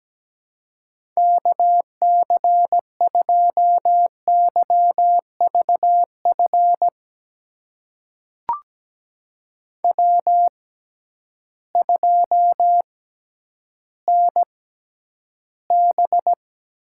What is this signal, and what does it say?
Morse code, heard as a keyed tone, about 17 wpm, spaced as for 10 wpm, 700 Hz. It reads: KC2YVF W2NB
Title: USA Call Signs - 6x Character Spacing - 17wpm